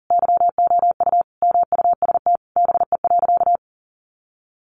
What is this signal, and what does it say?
Morse code, heard as a keyed tone, about 40 wpm, 700 Hz. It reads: YOU MUST BE.
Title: Sentences from Top 1400 Words 40wpm